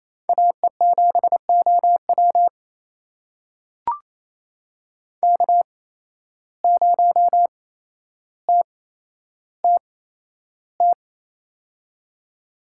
Morse code, heard as a keyed tone, about 29 wpm, spaced as for 12 wpm, 700 Hz. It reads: AE7OW K0TTT
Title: USA Call Signs - 5 Characters - 8x Character Spacing - 28wpm